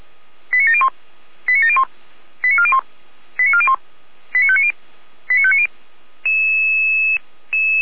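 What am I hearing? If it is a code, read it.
Five-tone selective call (ZVEI), heard as quick runs of five tones, tones of about 70 ms. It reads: 89701 89701 89401 89401 89590 89590
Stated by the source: ZVEI five-tone selective call received on SDR.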